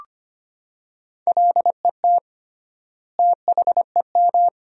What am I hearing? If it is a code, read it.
Morse code, heard as a keyed tone, about 25 wpm, 700 Hz. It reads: LET THEM
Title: 3x Word-Spacing - Sentences from Top 300 Words - 25wpm